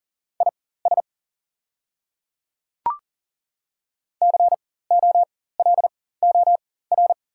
Morse code, heard as a keyed tone, about 40 wpm, spaced as for 19 wpm, 690 Hz. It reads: IS COLOR
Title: 4x Character Spacing - Top 300 Words - 40wpm